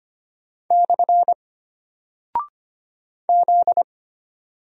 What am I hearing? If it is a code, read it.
Morse code, heard as a keyed tone, about 25 wpm, 700 Hz. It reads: / Z
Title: Single Character (Letter:Number:Punctuation:Prosign) Alt 25wpm